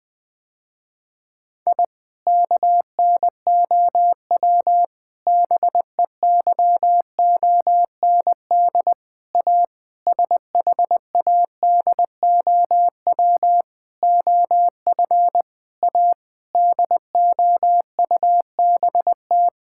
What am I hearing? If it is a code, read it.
Morse code, heard as a keyed tone, about 21 wpm, 700 Hz. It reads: I KNOW BEYOND A SHADOW OF A DOUBT
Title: Sentences with 7-Word Phrases - Spoken Beginning - 20wpm